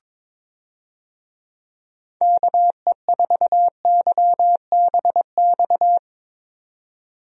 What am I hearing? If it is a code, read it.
Morse code, heard as a keyed tone, about 23 wpm, 700 Hz. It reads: KE4YBX
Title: USA Call Signs - 6 Characters - 22wpm